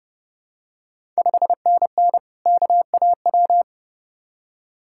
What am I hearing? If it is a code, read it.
Morse code, heard as a keyed tone, about 29 wpm, 700 Hz. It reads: 5NN KAW
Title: Ontario QSO Party Exchange 30wpm